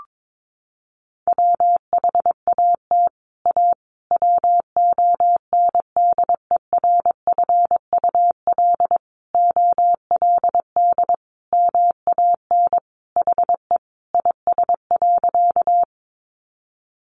Morse code, heard as a keyed tone, about 22 wpm, 700 Hz. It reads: WHAT A WONDERFUL OLD MAN HE IS.
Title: Sentences from Top 1100 Words 22wpm